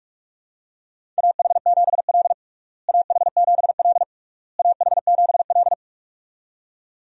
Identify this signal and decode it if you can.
Morse code, heard as a keyed tone, about 46 wpm, 700 Hz. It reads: AH7L AH7L AH7L
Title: USA Call Signs - 4 Characters - Repeated 3x - 45wpm